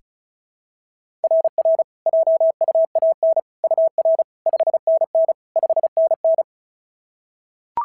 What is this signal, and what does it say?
Morse code, heard as a keyed tone, about 36 wpm, 690 Hz. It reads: RR JUAN UR 5NN 5NN
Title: SOTA QSO Practice 35wpm